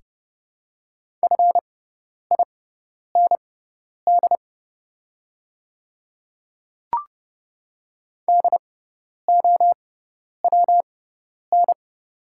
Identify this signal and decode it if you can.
Morse code, heard as a keyed tone, about 30 wpm, spaced as for 10 wpm, 700 Hz. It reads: FIND DOWN
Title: 6x Character Spacing - Top 100 Words - 30wpm